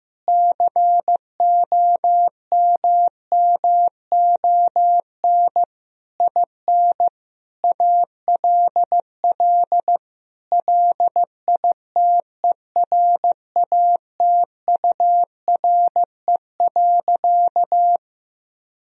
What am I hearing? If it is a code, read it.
Morse code, heard as a keyed tone, about 15 wpm, 700 Hz. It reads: COMMON IN ALL LITERATURE.
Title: Sentences from Top 1000 Words - Review Part 3 - 15wpm